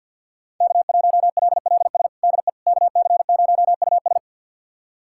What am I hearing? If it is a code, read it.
Morse code, heard as a keyed tone, about 49 wpm, 690 Hz. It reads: K1LLS DE KC0US
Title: USA Callsign DE Callsign - 5 Characters - 50wpm